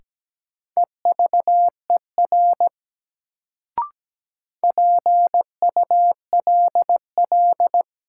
Morse code, heard as a keyed tone, about 17 wpm, 700 Hz. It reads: EVER PULL
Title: Sentences from Top 400 Words - New Word Drill - 17wpm